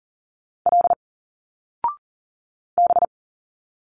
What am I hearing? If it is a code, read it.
Morse code, heard as a keyed tone, about 40 wpm, 700 Hz. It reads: L B
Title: Single Letters 40wpm